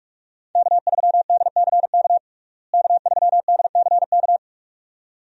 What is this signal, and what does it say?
Morse code, heard as a keyed tone, about 46 wpm, 700 Hz. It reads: K3DCK K3DCK
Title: USA Call Signs - 5 Characters - Repeated 2x - 45wpm